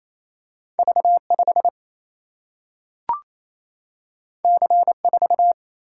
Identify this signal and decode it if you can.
Morse code, heard as a keyed tone, about 28 wpm, 700 Hz. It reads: V 5 C 4
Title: Single-Letter Single-Number 28wpm